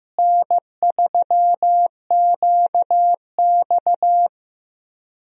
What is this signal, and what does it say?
Morse code, heard as a keyed tone, about 16 wpm, 700 Hz. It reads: N3QX
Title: USA Call Signs - 4 Characters - Repeated 3x - 15wpm